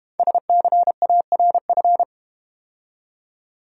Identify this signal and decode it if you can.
Morse code, heard as a keyed tone, about 32 wpm, 700 Hz. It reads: SCARF